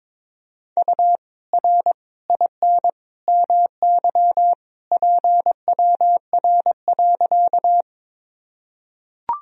Morse code, heard as a keyed tone, about 22 wpm, 700 Hz. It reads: U R IN MY PWR.
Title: Sentences from Top 500 Words and Abbreviations 22wpm